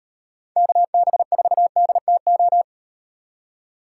Morse code, heard as a keyed tone, about 36 wpm, 700 Hz. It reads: KB4DTO